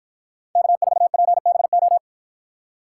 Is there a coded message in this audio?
Yes: Morse code from 0.6 to 2.0 s, heard as a keyed tone, about 53 wpm, 700 Hz. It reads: K4PBO